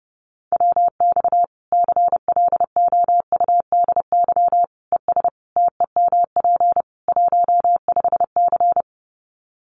Morse code, heard as a keyed tone, about 28 wpm, 700 Hz. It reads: WX CLOUDY ES TEMP 15C
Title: Common QSO Phrases 30wpm